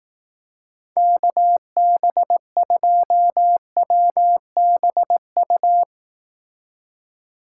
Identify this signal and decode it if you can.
Morse code, heard as a keyed tone, about 18 wpm, 700 Hz. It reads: KB2WBU
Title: Speed Racing - USA Call Signs 15wpm